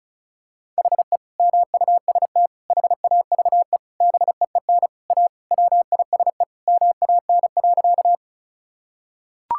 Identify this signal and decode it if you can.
Morse code, heard as a keyed tone, about 36 wpm, 700 Hz. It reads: HE MUST HAVE BEEN A WISE MAN.
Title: Sentences from Top 1600 Words 35wpm